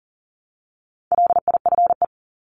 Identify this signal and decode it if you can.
Morse code, heard as a keyed tone, about 40 wpm, 700 Hz. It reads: LIFE
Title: Top 500 Words 40wpm